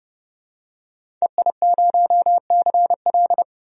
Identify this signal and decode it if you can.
Morse code, heard as a keyed tone, about 30 wpm, 700 Hz. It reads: EI0CL
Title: DXCC-100 Call Signs - 5 Characters - 30wpm